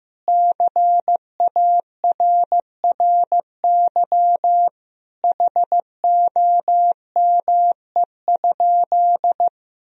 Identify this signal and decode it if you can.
Morse code, heard as a keyed tone, about 15 wpm, 700 Hz. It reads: CARRY HOME?
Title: Sentences from Top 1000 Words - Review Part 1 - 15wpm